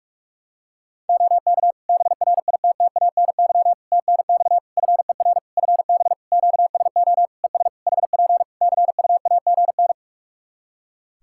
Morse code, heard as a keyed tone, about 46 wpm, 700 Hz. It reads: OK BRITTANY TNX FER FB QSO ES HP CUAGN
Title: K1USN SST QSO Practice 45wpm